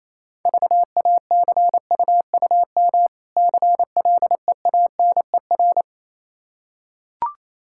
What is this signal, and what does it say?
Morse code, heard as a keyed tone, about 27 wpm, 690 Hz. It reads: VACUUM CLEANER
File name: Easy Words - Speed-Racing with Hints - 22wpm